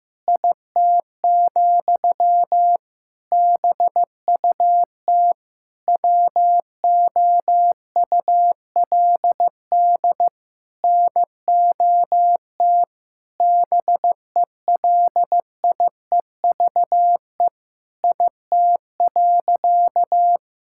Morse code, heard as a keyed tone, about 16 wpm, 700 Hz. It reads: IT, BUT WOULD NOT BELIEVE IT.
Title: Sentences from Top 1000 Words - Review Part 6 - 15wpm